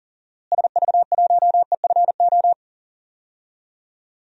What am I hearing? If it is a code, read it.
Morse code, heard as a keyed tone, about 40 wpm, 700 Hz. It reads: SV1EFO
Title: DXCC-100 Call Signs - 6 Characters - Repeated 3x - 40wpm